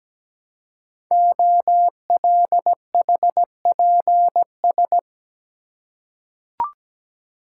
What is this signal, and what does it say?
Morse code, heard as a keyed tone, about 17 wpm, 700 Hz. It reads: OLHPS
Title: Code Groups - 5 Letters - 17wpm